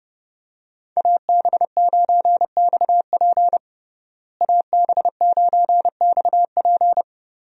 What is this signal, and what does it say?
Morse code, heard as a keyed tone, about 32 wpm, 700 Hz. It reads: AB9XP AB9XP
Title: USA Call Signs - 5 Characters - Repeated 2x - 30wpm